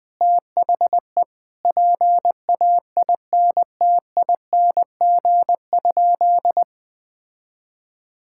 Morse code, heard as a keyed tone, about 20 wpm, 700 Hz. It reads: THE PAINTING?
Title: Sentences from Top 1000 Words - Review Part 6 - 20wpm